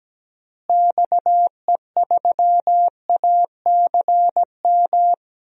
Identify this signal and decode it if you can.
Morse code, heard as a keyed tone, about 17 wpm, 700 Hz. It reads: XE3ACM
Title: DXCC-100 Call Signs - 6 Characters - Repeated 3x - 17wpm